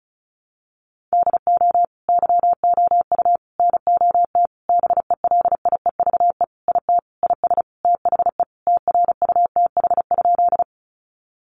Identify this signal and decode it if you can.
Morse code, heard as a keyed tone, about 34 wpm, 700 Hz. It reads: DO YOU NOT BELIEVE IT IS THE TRUTH?
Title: Sentences from Top 1000 Words 35wpm